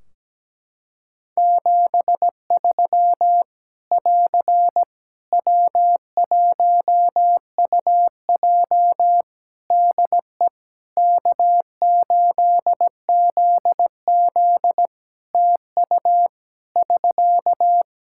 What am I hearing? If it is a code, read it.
Morse code, heard as a keyed tone, about 17 wpm, 700 Hz. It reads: 73 <AR> W1UJ DE K8ZZ TU <SK>
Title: Practice QSO - Exchange 4 - Ending - 17wpm